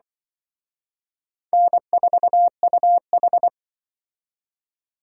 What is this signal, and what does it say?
Morse code, heard as a keyed tone, about 24 wpm, 700 Hz. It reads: N4UH
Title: Speed Racing - USA 1x2 Call Signs 20wpm